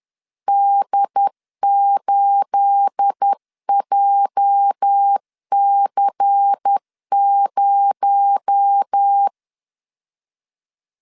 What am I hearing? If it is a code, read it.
Morse code, heard as a keyed tone, about 10 wpm, 800 Hz. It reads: D8JC0